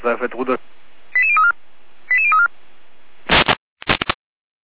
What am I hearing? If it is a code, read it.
Five-tone selective call (ZVEI), heard as quick runs of five tones, tones of about 80 ms, with speech. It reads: 88034 88034